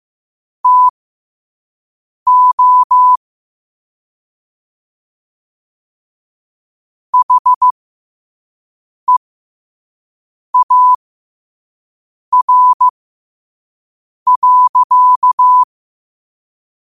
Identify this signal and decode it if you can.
Morse code, heard as a keyed tone, about 15 wpm, spaced as for 5 wpm, 1000 Hz. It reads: TO HEAR.